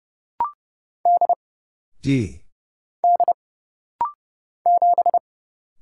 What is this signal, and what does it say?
Morse code, heard as a keyed tone, about 30 wpm, 700 Hz. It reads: D D 7
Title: Single Letter-Number - Rapid-Fire 30wpm